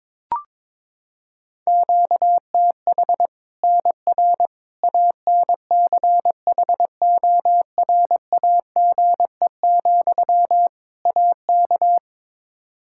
Morse code, heard as a keyed tone, about 22 wpm, 700 Hz. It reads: QTH NR ANCHORAGE, AK
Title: K1USN SST QSO Practice 22wpm